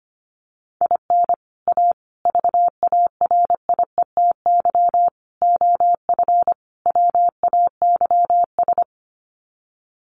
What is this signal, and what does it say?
Morse code, heard as a keyed tone, about 25 wpm, 700 Hz. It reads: IN A VARIETY OF WAYS
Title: Sets of 5 Words Encore 25wpm